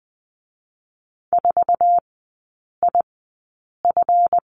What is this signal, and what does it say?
Morse code, heard as a keyed tone, about 19 wpm, spaced as for 8 wpm, 690 Hz. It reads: 4IF
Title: 159 Lesson 26e - Letter g - Three Characters 20wpm